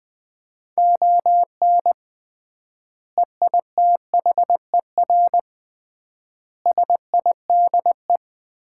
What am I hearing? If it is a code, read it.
Morse code, heard as a keyed tone, about 20 wpm, 700 Hz. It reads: ON EITHER SIDE
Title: 3x Word-Spacing - Sets of 5 Words - 20wpm